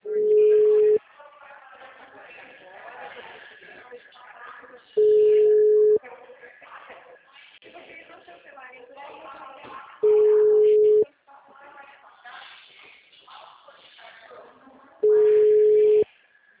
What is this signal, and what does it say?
Ringback tone (Europe, 425 Hz), with speech.
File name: out- -2001- - - .52444